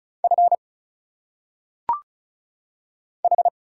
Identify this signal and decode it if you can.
Morse code, heard as a keyed tone, about 33 wpm, 700 Hz. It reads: F H
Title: Single Letters - Phonetic Pronunciation 35wpm